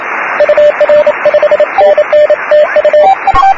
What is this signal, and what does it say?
Morse code, heard as a keyed tone, about 28 wpm, 600 Hz, with noise. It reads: UR 5NN TU
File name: satellite / 06-0824PVF